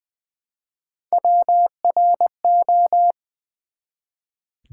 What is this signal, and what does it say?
Morse code, heard as a keyed tone, about 20 wpm, 690 Hz. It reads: WRO
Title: Build A Word - 1 Letter At A Time - 5-Letter Words - 20wpm